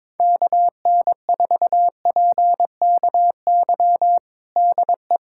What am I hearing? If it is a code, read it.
Morse code, heard as a keyed tone, about 22 wpm, 700 Hz. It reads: KN4PKY DE
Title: USA Callsign DE Callsign - 6 Characters - 22wpm